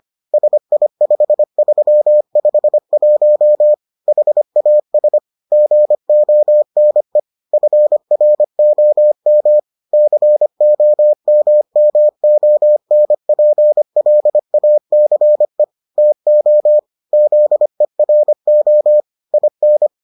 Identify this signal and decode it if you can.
Morse code, heard as a keyed tone, about 25 wpm, 600 Hz. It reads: SI5351 HAS GONE FROM COMMONPLACE TO ZERO IN